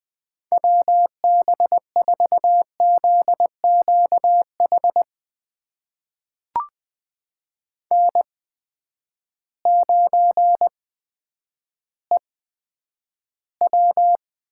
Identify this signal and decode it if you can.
Morse code, heard as a keyed tone, about 21 wpm, spaced as for 10 wpm, 700 Hz. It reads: WB4ZQH N9EW
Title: USA Call Signs - 8x Character Spacing - 20wpm